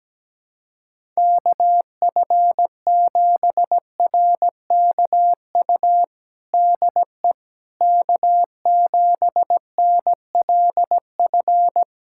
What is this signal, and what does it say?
Morse code, heard as a keyed tone, about 17 wpm, 700 Hz. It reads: KF7RKU DE K7NLF
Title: USA Callsign DE Callsign 17wpm